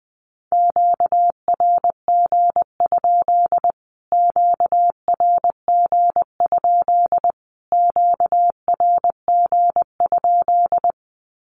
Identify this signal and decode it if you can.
Morse code, heard as a keyed tone, about 20 wpm, 700 Hz. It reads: QRG? QRG? QRG?
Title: Q-Codes Repeated 3x 20wpm